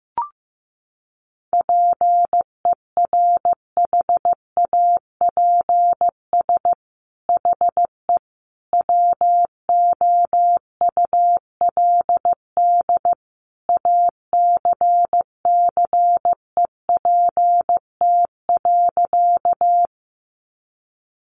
Morse code, heard as a keyed tone, about 15 wpm, 700 Hz. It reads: PERHAPS HE WOULD ACCEPT.
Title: Sentences from Top 1300 Words 15wpm